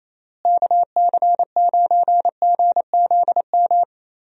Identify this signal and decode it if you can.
Morse code, heard as a keyed tone, about 27 wpm, 700 Hz. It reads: KC9GZM
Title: USA Call Signs - 6 Characters - Repeated 3x - 28wpm